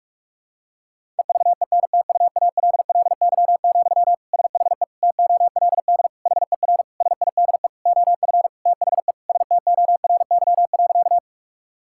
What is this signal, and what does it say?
Morse code, heard as a keyed tone, about 46 wpm, 700 Hz. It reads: EVENTUALLY, SHE TOLD HER SIDE OF THE STORY.
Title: Sentences from Top 1000 Words - Review Part 6 - 45wpm